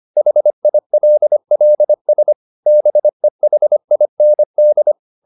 Morse code, heard as a keyed tone, about 25 wpm, 600 Hz. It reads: HILLS BEHIND
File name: the old man and the sea part 1 25wpm